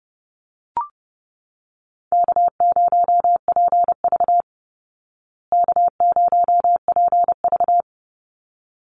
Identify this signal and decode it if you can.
Morse code, heard as a keyed tone, about 30 wpm, 700 Hz. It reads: K0PV K0PV
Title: Call Signs Repeated 3x - 30wpm